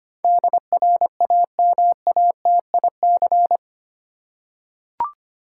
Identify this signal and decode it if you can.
Morse code, heard as a keyed tone, about 25 wpm, 700 Hz. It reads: DRAMATIC
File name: Eight Letter Words - 25wpm